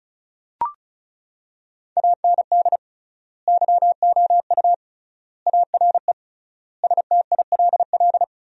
Morse code, heard as a keyed tone, about 36 wpm, 700 Hz. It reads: AND YOU ARE STILL
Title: 3x Word-Spacing - Sentences from Top 400 Words - 35wpm